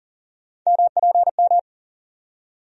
Morse code, heard as a keyed tone, about 40 wpm, 700 Hz. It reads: MPM
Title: USA Call Sign Postfixes - 3 Letters 40wpm